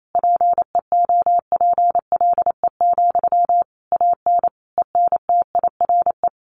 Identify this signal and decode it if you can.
Morse code, heard as a keyed tone, about 28 wpm, 700 Hz. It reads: PEOPLE, AN ENTIRE
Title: A Sound Of Thunder 28wpm